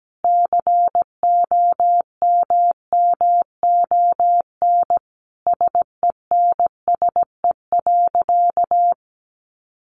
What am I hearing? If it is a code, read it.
Morse code, heard as a keyed tone, about 17 wpm, 700 Hz. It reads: COMMON SENSE.
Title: Sentences from Top 1500 Words 17wpm